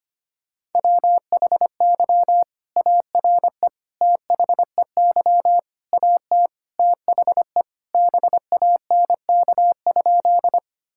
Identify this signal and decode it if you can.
Morse code, heard as a keyed tone, about 25 wpm, 700 Hz. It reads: WHY ARE THEY AT THE BANK?
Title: Sentences from Top 1000 Words - Review Part 6 - 25wpm